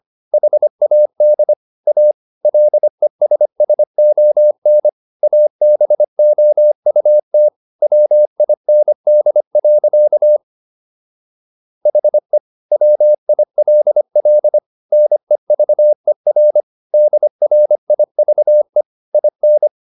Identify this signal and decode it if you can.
Morse code, heard as a keyed tone, about 25 wpm, 600 Hz. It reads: HAD A LESSON ABOUT WIND. HE WILL NEVER DRIVE IN